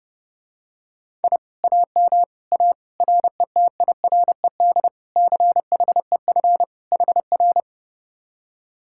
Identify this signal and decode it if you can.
Morse code, heard as a keyed tone, about 29 wpm, 700 Hz. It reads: I AM A RETIRED CHEF HR